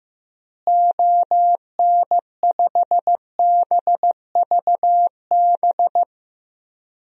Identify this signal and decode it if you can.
Morse code, heard as a keyed tone, about 14 wpm, 700 Hz. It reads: ON5BVB
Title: DXCC-100 Call Signs - 6 Characters - Repeated 3x - 15wpm